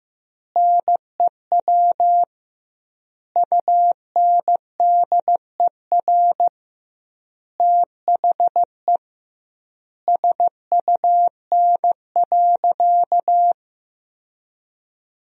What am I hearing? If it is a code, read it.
Morse code, heard as a keyed tone, about 16 wpm, 700 Hz. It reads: NEW UNDER THE SUN.
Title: 2x Word-Spacing - Sentences from Top 500 Words - 15wpm